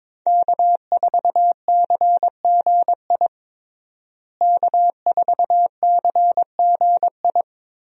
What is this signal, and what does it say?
Morse code, heard as a keyed tone, about 22 wpm, 700 Hz. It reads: K4CGI K4CGI
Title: USA Call Signs - 5 Characters - Repeated 3x - 22wpm